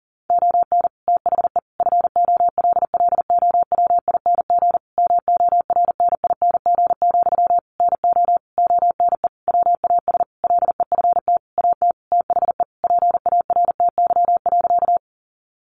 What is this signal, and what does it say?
Morse code, heard as a keyed tone, about 40 wpm, 700 Hz. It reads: ON THE FOLLOWING MORNING, NO ONE WAS LEFT AT THE PARTY.
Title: Sentences from Top 1400 Words 40wpm